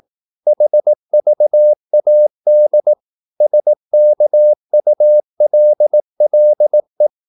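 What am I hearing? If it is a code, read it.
Morse code, heard as a keyed tone, about 18 wpm, 600 Hz. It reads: HVAD SKULLE
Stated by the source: Slow (QRS) Morse practice at 18 wpm, CW med Gnister.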